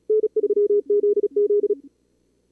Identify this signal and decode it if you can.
Morse code, heard as a keyed tone, about 39 wpm, 440 Hz. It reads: N3ZZ
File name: CWSprint / Feb2022 / k5tr-feb-na-sprint-cw-k5ot -2300